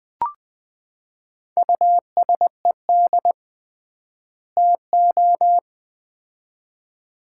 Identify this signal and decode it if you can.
Morse code, heard as a keyed tone, about 20 wpm, 700 Hz. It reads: USED TO
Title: 3x Word-Spacing - Sets of 2 Words - 20wpm